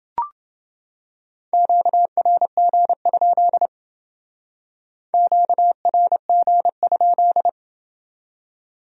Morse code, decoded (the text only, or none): QRG? QRG?